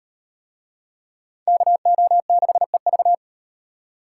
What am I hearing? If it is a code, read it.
Morse code, heard as a keyed tone, about 38 wpm, 700 Hz. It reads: KO6EV